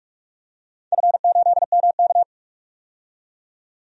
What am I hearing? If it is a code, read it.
Morse code, heard as a keyed tone, about 47 wpm, 700 Hz. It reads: F8MK
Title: DXCC-100 Call Signs - 4 Characters - 45wpm